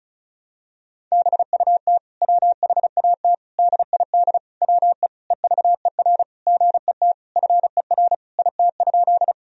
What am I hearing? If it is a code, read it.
Morse code, heard as a keyed tone, about 36 wpm, 700 Hz. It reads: BUT WHAT DID WE EVER GET FER IT?
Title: Sentences from Top 400 Words and Abbreviations 35wpm